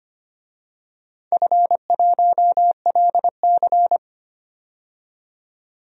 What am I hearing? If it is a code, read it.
Morse code, heard as a keyed tone, about 25 wpm, 700 Hz. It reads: F1LC
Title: DXCC-100 Call Signs - 4 Characters - 25wpm